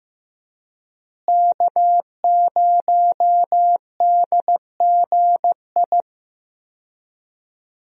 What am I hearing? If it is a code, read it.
Morse code, heard as a keyed tone, about 15 wpm, 700 Hz. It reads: K0DGI
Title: USA Call Signs - 5 Characters - Repeated 2x - 15wpm